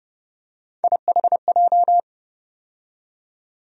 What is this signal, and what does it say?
Morse code, heard as a keyed tone, about 29 wpm, 700 Hz. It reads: IHJ
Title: USA Call Sign Postfixes - 3 Letters 30wpm